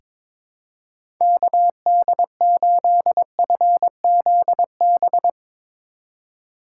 Morse code, heard as a keyed tone, about 22 wpm, 700 Hz. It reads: KD8FZB
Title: USA Call Signs - 6 Characters - 3x Character Spacing - 22wpm